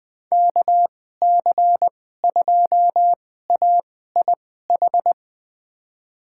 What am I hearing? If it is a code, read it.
Morse code, heard as a keyed tone, about 20 wpm, spaced as for 14 wpm, 700 Hz. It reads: KC2AIH
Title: USA Call Signs - 6 Characters - 2x Character Spacing - 20wpm